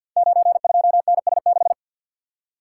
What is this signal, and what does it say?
Morse code, heard as a keyed tone, about 53 wpm, 700 Hz. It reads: 92NS6
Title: Code Groups - 5 Letter:Numbers - 50wpm